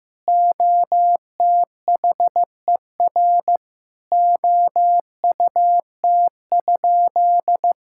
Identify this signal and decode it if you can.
Morse code, heard as a keyed tone, about 14 wpm, 700 Hz. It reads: OTHER OUT?
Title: Sentences from Top 1000 Words - Review Part 6 - 15wpm